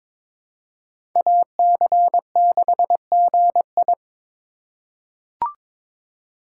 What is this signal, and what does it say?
Morse code, heard as a keyed tone, about 22 wpm, 700 Hz. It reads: AC6GI
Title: USA Call Signs - 5 Characters - 3x Character Spacing - 22wpm